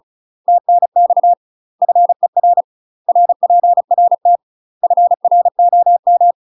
Morse code, heard as a keyed tone, about 36 wpm, 700 Hz. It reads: TNX FER RPRT FROM
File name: UK MASTER QSOs TRAINING / 35WPM / UK MASTER QSOs 0038